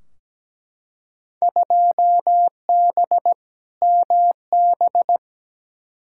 Morse code, heard as a keyed tone, about 17 wpm, 700 Hz. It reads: 2B MB
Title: Field Day Exchanges Encore - 17wpm